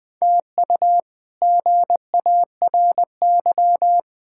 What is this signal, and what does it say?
Morse code, heard as a keyed tone, about 21 wpm, 700 Hz. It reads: TU GARY
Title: K1USN SST QSO Practice - 20wpm